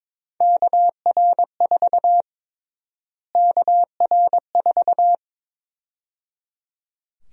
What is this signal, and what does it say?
Morse code, heard as a keyed tone, about 22 wpm, 700 Hz. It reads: KR4 KR4
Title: USA Call Sign Prefixes - Repeated 2x - 22wpm